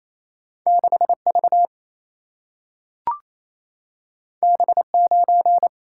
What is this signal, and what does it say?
Morse code, heard as a keyed tone, about 28 wpm, 700 Hz. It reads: 6 V B 9
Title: Single-Letter Single-Number 28wpm